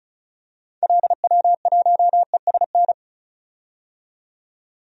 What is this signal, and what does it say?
Morse code, heard as a keyed tone, about 34 wpm, 700 Hz. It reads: LW1ESN
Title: DXCC-100 Call Signs - 6 Characters - 35wpm